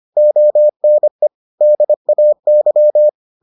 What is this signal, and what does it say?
Morse code, heard as a keyed tone, about 25 wpm, 600 Hz. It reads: ONE DAY